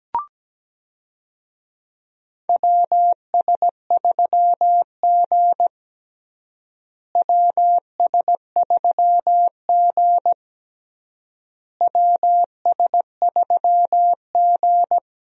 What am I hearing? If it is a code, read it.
Morse code, heard as a keyed tone, about 17 wpm, 700 Hz. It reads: WS3G WS3G WS3G
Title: USA Call Signs - 4 Characters - Repeated 3x - 17wpm